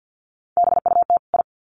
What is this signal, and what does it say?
Morse code, heard as a keyed tone, about 50 wpm, 700 Hz. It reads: BUT I
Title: Sets of 2 Words 50wpm